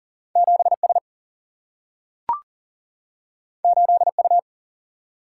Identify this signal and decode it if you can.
Morse code, heard as a keyed tone, about 40 wpm, 700 Hz. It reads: 7S 8U
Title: Single-Letter Single-Number 40wpm